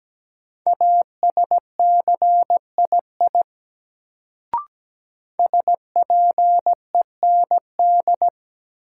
Morse code, heard as a keyed tone, about 17 wpm, 700 Hz. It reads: ASCII SPEND
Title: Sets of 5 Words - New Word Drill - 17wpm